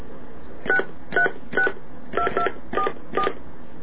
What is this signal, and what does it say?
DTMF, heard as telephone touch tones, tones of about 90 ms. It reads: 6362244